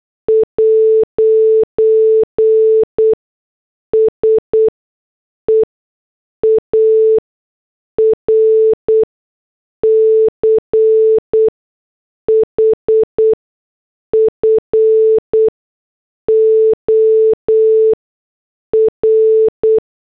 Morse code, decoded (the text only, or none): 'SEARCH FOR